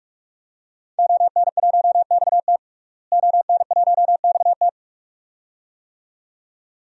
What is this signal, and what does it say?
Morse code, heard as a keyed tone, about 47 wpm, 700 Hz. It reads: ON1XT ON1XT
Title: DXCC-100 Call Signs - 5 Characters - Repeated 2x - 45wpm